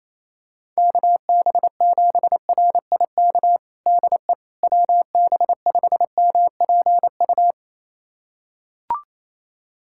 Morse code, heard as a keyed tone, about 28 wpm, 700 Hz. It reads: KB7RIK DE WB5MPU
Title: USA Callsign DE Callsign - 6 Characters - 28wpm